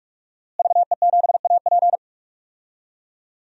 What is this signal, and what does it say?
Morse code, heard as a keyed tone, about 42 wpm, 700 Hz. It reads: VE7AP
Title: DXCC-100 Call Signs - 5 Characters - Repeated 3x - 45wpm